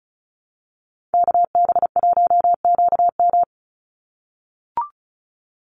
Morse code, heard as a keyed tone, about 34 wpm, 700 Hz. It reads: KB1QM
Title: USA Call Signs 35wpm